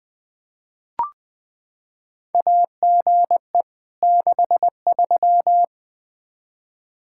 Morse code, heard as a keyed tone, about 20 wpm, 700 Hz. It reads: AGE 63
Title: K1USN SST QSO Practice 20wpm